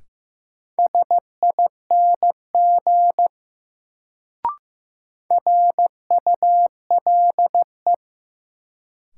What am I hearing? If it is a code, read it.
Morse code, heard as a keyed tone, about 15 wpm, 700 Hz. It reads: SING RULE
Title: Sentences from Top 400 Words - New Word Drill - 15wpm